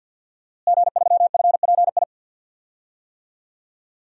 Morse code, heard as a keyed tone, about 51 wpm, 700 Hz. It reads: G3FPI
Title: DXCC-100 Call Signs - 5 Characters - Repeated 2x - 50wpm